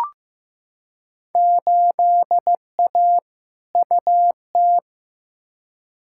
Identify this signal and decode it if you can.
Morse code, heard as a keyed tone, about 15 wpm, 700 Hz. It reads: 8A UT
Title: Field Day Exchanges Encore - 15wpm